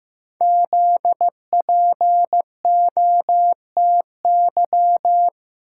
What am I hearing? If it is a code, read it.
Morse code, heard as a keyed tone, about 15 wpm, 700 Hz. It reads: ZPOTY
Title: Code Groups - 5 Letters - 15wpm